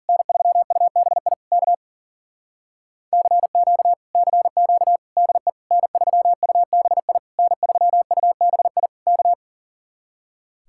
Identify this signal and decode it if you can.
Morse code, heard as a keyed tone, about 50 wpm, 700 Hz. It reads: N3UBI K CQ CQ DE N3UBI N3UBI K